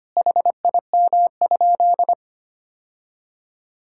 Morse code, heard as a keyed tone, about 25 wpm, 700 Hz. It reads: HIM?
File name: practice / 25wpm / Sentences with 2-Word Phrases - 25wpm